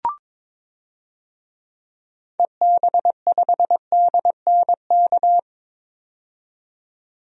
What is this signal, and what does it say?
Morse code, heard as a keyed tone, about 22 wpm, 700 Hz. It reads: EB5DNK
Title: DXCC-100 Call Signs - 6 Characters - 22wpm